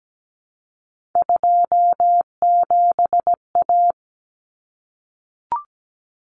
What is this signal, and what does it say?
Morse code, heard as a keyed tone, about 17 wpm, 700 Hz. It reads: 27A
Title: Common Cut Numbers - 3-Digit 17wpm